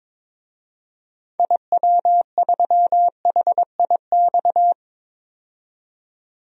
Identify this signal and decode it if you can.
Morse code, heard as a keyed tone, about 22 wpm, 700 Hz. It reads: IW3HIX
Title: DXCC-100 Call Signs - 6 Characters - 22wpm